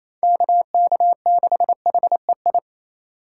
Morse code, heard as a keyed tone, about 35 wpm, 700 Hz. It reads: KK6HEI
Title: Speed Racing - USA Call Signs 28wpm